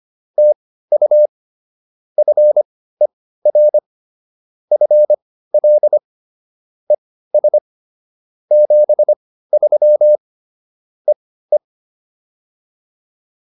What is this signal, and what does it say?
Morse code, heard as a keyed tone, about 25 wpm, spaced as for 15 wpm, 600 Hz. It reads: TU FER FL ES 73 EE
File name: POTA203 15